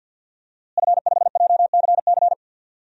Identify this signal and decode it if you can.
Morse code, heard as a keyed tone, about 50 wpm, 700 Hz. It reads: F5JCC
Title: DXCC-100 Call Signs - 5 Characters - Repeated 3x - 50wpm